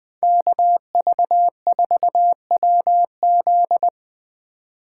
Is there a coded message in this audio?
Yes: Morse code from 0.2 to 3.9 s, heard as a keyed tone, about 21 wpm, 700 Hz. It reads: KV4WZ